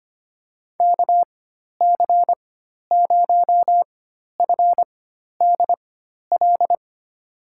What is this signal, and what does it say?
Morse code, heard as a keyed tone, about 25 wpm, spaced as for 12 wpm, 700 Hz. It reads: KC0FDL
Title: USA Call Signs - 6 Characters - 4x Character Spacing - 25wpm